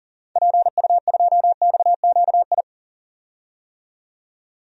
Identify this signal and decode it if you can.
Morse code, heard as a keyed tone, about 40 wpm, 700 Hz. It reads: PU2XQI
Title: DXCC-100 Call Signs - 6 Characters - 40wpm